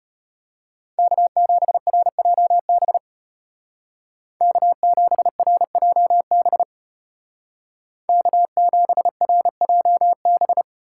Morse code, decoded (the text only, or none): K7RJB K7RJB K7RJB